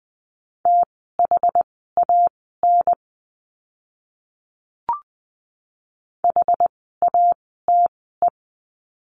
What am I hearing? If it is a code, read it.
Morse code, heard as a keyed tone, about 20 wpm, spaced as for 14 wpm, 700 Hz. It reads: THAN HATE
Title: 068 Lesson 12f - Letter d - Words 20wpm